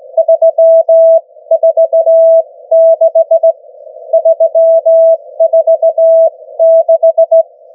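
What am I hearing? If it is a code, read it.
Morse code, heard as a keyed tone, about 17 wpm, spaced as for 14 wpm, 660 Hz. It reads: 346 346
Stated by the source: Morse from the New Zealand Net, the national amateur radio net of Aotearoa/New Zealand, meeting daily on 3535.0 kHz.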